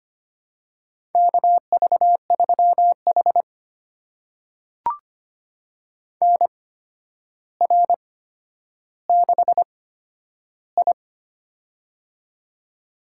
Morse code, heard as a keyed tone, about 25 wpm, spaced as for 11 wpm, 700 Hz. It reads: KV3H NR6I
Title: USA Call Signs - 4 Characters - 8x Character Spacing - 25wpm